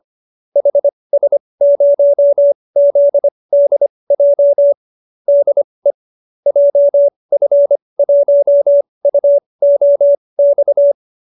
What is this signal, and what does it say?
Morse code, heard as a keyed tone, about 25 wpm, spaced as for 20 wpm, 590 Hz. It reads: HS0ZDJ DE JF1UOX